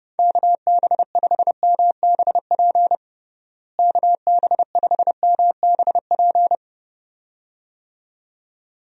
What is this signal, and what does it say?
Morse code, heard as a keyed tone, about 30 wpm, 700 Hz. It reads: KB5MBP KB5MBP
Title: USA Call Signs - 6 Characters - Repeated 2x - 30wpm